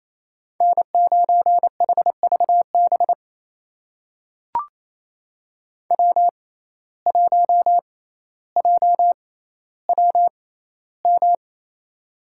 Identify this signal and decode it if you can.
Morse code, heard as a keyed tone, about 27 wpm, spaced as for 14 wpm, 700 Hz. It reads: N9HVB W1JWM
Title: USA Call Signs - 6x Character Spacing - 28wpm